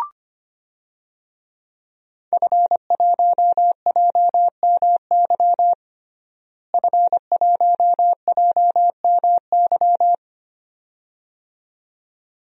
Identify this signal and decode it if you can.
Morse code, heard as a keyed tone, about 25 wpm, 700 Hz. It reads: F1JMY F1JMY
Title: DXCC-100 Call Signs - 5 Characters - Repeated 2x - 25wpm